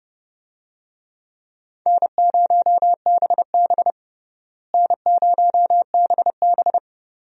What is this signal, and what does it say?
Morse code, heard as a keyed tone, about 32 wpm, 700 Hz. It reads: N0BB N0BB
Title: USA Call Signs - 4 Characters - Repeated 3x - 30wpm